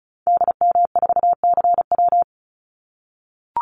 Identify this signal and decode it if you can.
Morse code, heard as a keyed tone, about 34 wpm, 700 Hz. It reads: DM4CW
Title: Call Signs - 35wpm